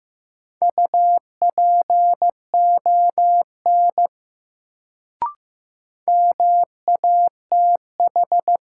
Morse code, heard as a keyed tone, about 15 wpm, 700 Hz. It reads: UPON MATH
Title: Sentences from Top 1000 Words - Review Part 6 - New Word Drill - 15wpm